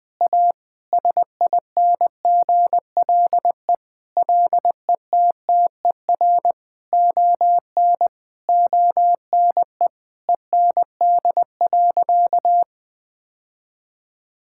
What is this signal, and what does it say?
Morse code, heard as a keyed tone, about 20 wpm, 700 Hz. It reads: A SINGLE LETTER ON ONE END.
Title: Sentences from Top 1600 Words 20wpm